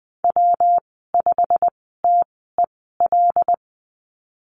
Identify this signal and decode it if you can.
Morse code, heard as a keyed tone, about 20 wpm, spaced as for 15 wpm, 700 Hz. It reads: W5TEL
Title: 167 Lesson 27g - Number 7 - Call Signs 20wpm